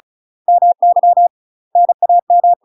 Morse code, heard as a keyed tone, about 36 wpm, 700 Hz. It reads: MY NAM
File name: UK MASTER QSOs TRAINING / 35WPM / UK MASTER QSOs 0044